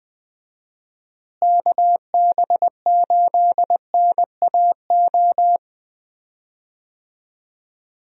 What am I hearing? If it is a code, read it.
Morse code, heard as a keyed tone, about 20 wpm, 700 Hz. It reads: KB8NAO
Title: USA Call Signs - 6 Characters - Repeated 2x - 20wpm